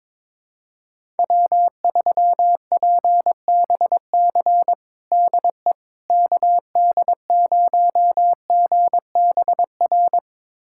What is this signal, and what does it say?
Morse code, heard as a keyed tone, about 22 wpm, 700 Hz. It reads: W3PBC DE KD0GBR
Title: USA Callsign DE Callsign 22wpm